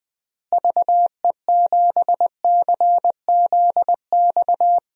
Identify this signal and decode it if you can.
Morse code, heard as a keyed tone, about 21 wpm, 700 Hz. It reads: VE7CZX
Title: DXCC-100 Call Signs - 6 Characters - Repeated 2x - 20wpm